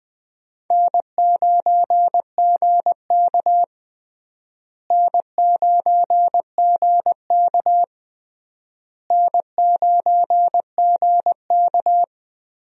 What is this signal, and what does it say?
Morse code, heard as a keyed tone, about 21 wpm, 700 Hz. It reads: N9GK N9GK N9GK
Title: USA Call Signs - 4 Characters - Repeated 3x - 20wpm